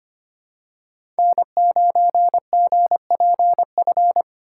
Morse code, heard as a keyed tone, about 25 wpm, 700 Hz. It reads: N9GPF
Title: USA Call Signs - 4x Character Spacing - 25wpm